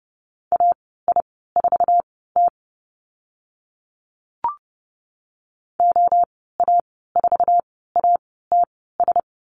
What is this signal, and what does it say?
Morse code, heard as a keyed tone, about 30 wpm, spaced as for 17 wpm, 700 Hz. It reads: AI4T OA4ATS
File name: Lesson 14g - Number 2 - Call Signs 30wpm